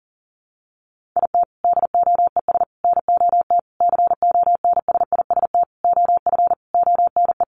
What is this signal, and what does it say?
Morse code, heard as a keyed tone, about 37 wpm, 700 Hz. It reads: IT DOES NOT CONSIST OF ONE
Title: Sentences from Top 1400 Words 40wpm